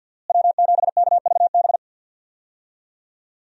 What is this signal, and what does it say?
Morse code, heard as a keyed tone, about 47 wpm, 690 Hz. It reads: W7KVB
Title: USA Call Signs - 5 Characters - Repeated 3x - 50wpm